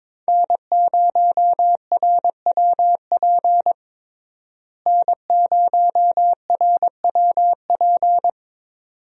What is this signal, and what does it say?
Morse code, heard as a keyed tone, about 22 wpm, 700 Hz. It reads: N0RWP N0RWP
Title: USA Call Signs - 5 Characters - Repeated 2x - 22wpm